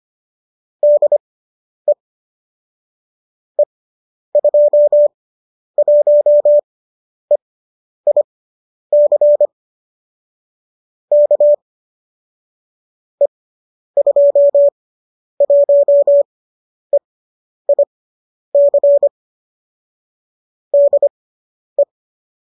Morse code, heard as a keyed tone, about 25 wpm, spaced as for 10 wpm, 590 Hz. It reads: DE E21EIC K E21EIC DE